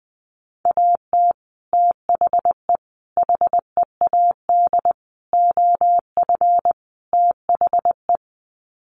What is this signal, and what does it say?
Morse code, heard as a keyed tone, about 19 wpm, 700 Hz. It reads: AT THE HEAD OF THE
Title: Sets of 5 Words 20wpm